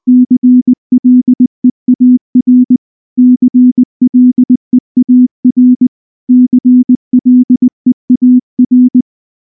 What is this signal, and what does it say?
Morse code, heard as a keyed tone, about 20 wpm, 250 Hz. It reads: CLEAR CLEAR CLEAR
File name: morse / morse-training-20WPM-6